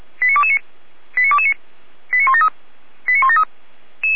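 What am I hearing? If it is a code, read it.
Five-tone selective call (ZVEI), heard as quick runs of five tones, tones of about 70 ms. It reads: 89229 89229 89173 89173
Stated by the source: ZVEI five-tone selective call received by SDR, from the Feuerwehr Orlamünde fire brigade audio server.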